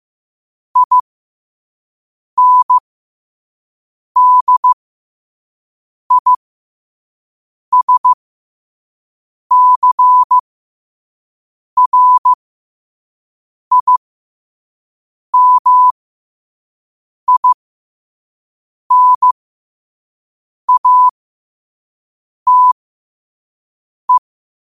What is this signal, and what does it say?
Morse code, heard as a keyed tone, about 15 wpm, spaced as for 5 wpm, 1000 Hz. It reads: INDISCRIMINATE
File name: qotdfiles / 5wpm